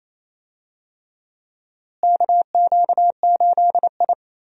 Morse code, heard as a keyed tone, about 28 wpm, 700 Hz. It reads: KQ8I
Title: USA Call Signs - 4 Characters - Repeated 3x - 28wpm